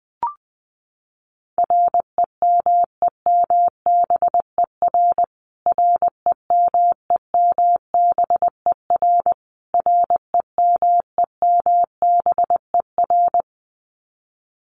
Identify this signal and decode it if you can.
Morse code, heard as a keyed tone, about 20 wpm, 700 Hz. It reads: REMEMBER REMEMBER REMEMBER
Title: Top 500 Words Repeated 20wpm